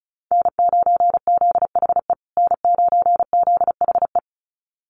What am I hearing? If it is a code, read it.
Morse code, heard as a keyed tone, about 34 wpm, 700 Hz. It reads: N9ZHE N9ZHE
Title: USA Call Signs Repeated 2x 35wpm